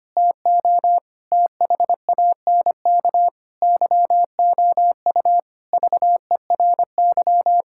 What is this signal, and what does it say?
Morse code, heard as a keyed tone, about 25 wpm, 700 Hz. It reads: TO THANK YOU VERY
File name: Sets of 9 Words - 25wpm